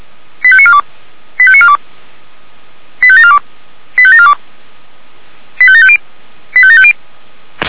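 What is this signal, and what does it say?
Five-tone selective call (ZVEI), heard as quick runs of five tones, tones of about 70 ms. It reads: 85942 85942 85742 85742 85750 85750
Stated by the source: ZVEI five-tone selective call received with an SDR.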